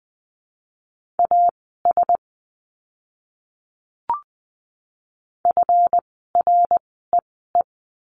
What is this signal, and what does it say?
Morse code, heard as a keyed tone, about 20 wpm, spaced as for 15 wpm, 700 Hz. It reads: AS FREE
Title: 139 Lesson 23f - Letter f - Words 20wpm